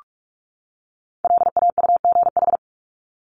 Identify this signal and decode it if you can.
Morse code, heard as a keyed tone, about 44 wpm, 700 Hz. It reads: LAUGH
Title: Top 500 Words 45wpm